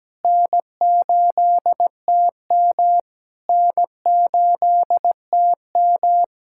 Morse code, heard as a keyed tone, about 17 wpm, 700 Hz. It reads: N8TM N8TM
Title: SOTA QSO Practice 17wpm